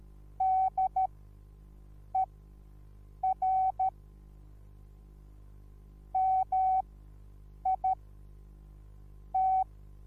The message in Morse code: DER MIT